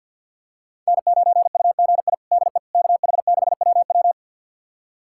Morse code, heard as a keyed tone, about 49 wpm, 700 Hz. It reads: N9UGI DE KH6WW
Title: USA Callsign DE Callsign - 5 Characters - 50wpm